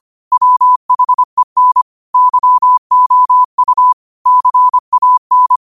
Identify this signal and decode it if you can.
Morse code, heard as a keyed tone, about 25 wpm, 1000 Hz. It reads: WHEN YOU CAN
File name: qotdfiles / 25wpm